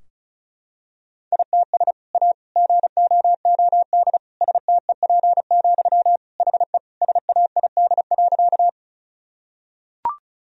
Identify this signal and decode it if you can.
Morse code, heard as a keyed tone, about 35 wpm, 700 Hz. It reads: ITS A GOOD STEP, HE SAID.
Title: Sentences from Top 1000 Words - Review Part 2 - 35wpm